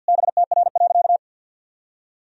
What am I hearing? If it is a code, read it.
Morse code, heard as a keyed tone, about 45 wpm, 700 Hz. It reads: BTR.
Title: Sentences from Top 400 Words and Abbreviations 50wpm